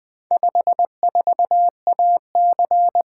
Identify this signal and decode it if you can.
Morse code, heard as a keyed tone, about 20 wpm, 700 Hz. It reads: 54AC
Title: Four Letter-Number Combinations - 20wpm